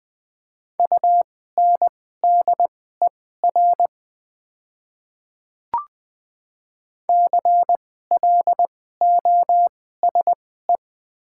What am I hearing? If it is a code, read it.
Morse code, heard as a keyed tone, about 20 wpm, spaced as for 14 wpm, 700 Hz. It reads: UNDER CLOSE
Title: 2x Character Spacing - Top 500 Words - 20wpm